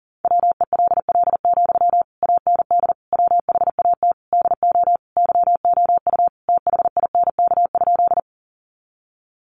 Morse code, heard as a keyed tone, about 40 wpm, 700 Hz. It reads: WELL, AND WHAT DO YOU THINK?
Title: Sentences from Top 200 Words 40wpm